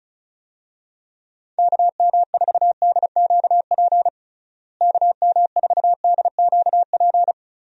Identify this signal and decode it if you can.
Morse code, heard as a keyed tone, about 34 wpm, 700 Hz. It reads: KM4DQP KM4DQP
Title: USA Call Signs - 6 Characters - Repeated 3x - 35wpm